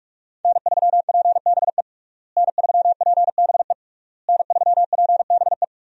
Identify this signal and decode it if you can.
Morse code, heard as a keyed tone, about 46 wpm, 700 Hz. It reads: N3PBE N3PBE N3PBE
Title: USA Call Signs - 5 Characters - Repeated 3x - 45wpm